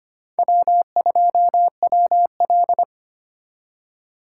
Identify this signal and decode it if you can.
Morse code, heard as a keyed tone, about 32 wpm, 700 Hz. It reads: W2WL